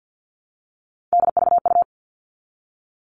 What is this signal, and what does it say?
Morse code, heard as a keyed tone, about 50 wpm, 700 Hz. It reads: DVU
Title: All Cut Numbers - 3-Digit 50wpm